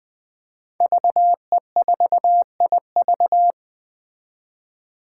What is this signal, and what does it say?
Morse code, heard as a keyed tone, about 20 wpm, 700 Hz. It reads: VE4IV
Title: Five Challenging Letter-Number Combinations 20wpm